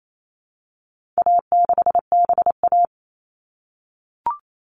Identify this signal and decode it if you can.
Morse code, heard as a keyed tone, about 28 wpm, 700 Hz. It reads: A6BA
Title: All Cut Numbers - 4-Digit 28wpm